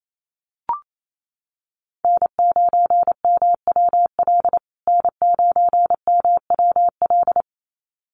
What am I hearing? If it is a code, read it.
Morse code, heard as a keyed tone, about 28 wpm, 700 Hz. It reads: N9MWL N9MWL
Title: USA Call Signs Repeated 2x 28wpm